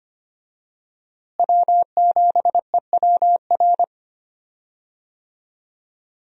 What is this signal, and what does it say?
Morse code, heard as a keyed tone, about 25 wpm, 700 Hz. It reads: W7EWR
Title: USA Call Signs - 5 Characters - Repeated 2x - 25wpm